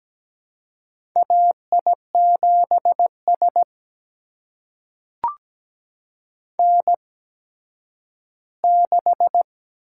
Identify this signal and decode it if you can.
Morse code, heard as a keyed tone, about 17 wpm, spaced as for 10 wpm, 700 Hz. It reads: AI7S N6
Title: USA Call Signs - 4 Characters - 8x Character Spacing - 17wpm